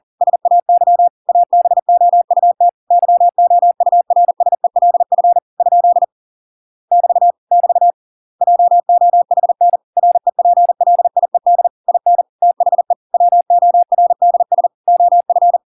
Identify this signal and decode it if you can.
Morse code, heard as a keyed tone, about 40 wpm, 700 Hz. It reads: SAY ABOUT YOURSELF ? <BT> <BT> JOHN REPLIED IN THE WORDS OF